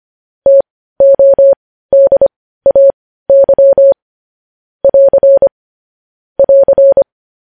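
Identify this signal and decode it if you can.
Morse code, heard as a keyed tone, about 25 wpm, spaced as for 15 wpm, 550 Hz. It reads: TODAY <AR> <AR>
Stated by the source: Morse news headlines practice sent at 15 WPM.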